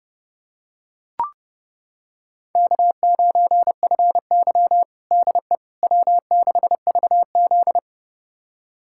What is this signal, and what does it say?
Morse code, heard as a keyed tone, about 32 wpm, 700 Hz. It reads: K9FY DE W6VZ
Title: USA Callsign DE Callsign - 4 Characters - 30wpm